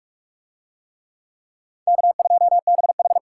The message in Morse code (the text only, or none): K2BH